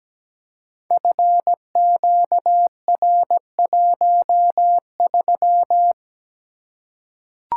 Morse code, heard as a keyed tone, about 17 wpm, 700 Hz. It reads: FQR13
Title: Code Groups - 5 Letter:Numbers - 17wpm